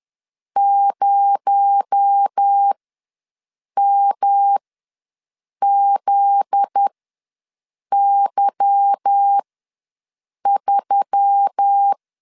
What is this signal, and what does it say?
Morse code, heard as a keyed tone, about 10 wpm, spaced as for 6 wpm, 800 Hz. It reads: 0MZY3
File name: LEZIONI / 10 SIGNS / 10A SLASH ws 10wpm 4121